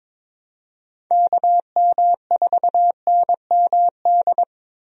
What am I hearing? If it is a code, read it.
Morse code, heard as a keyed tone, about 22 wpm, 700 Hz. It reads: KM4NMD
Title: USA Call Signs - 6 Characters - Repeated 2x - 22wpm